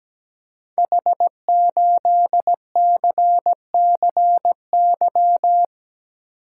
Morse code, heard as a keyed tone, about 17 wpm, 700 Hz. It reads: H8CCY
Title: Code Groups - 5 Letter:Numbers - 17wpm